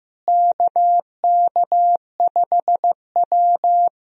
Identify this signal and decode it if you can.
Morse code, heard as a keyed tone, about 15 wpm, 700 Hz. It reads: KK5W
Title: USA Call Signs - 4 Characters - Repeated 2x - 15wpm